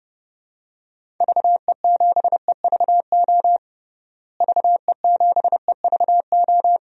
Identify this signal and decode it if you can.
Morse code, heard as a keyed tone, about 32 wpm, 700 Hz. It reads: VE7EVO VE7EVO
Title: DXCC-100 Call Signs - 6 Characters - Repeated 2x - 30wpm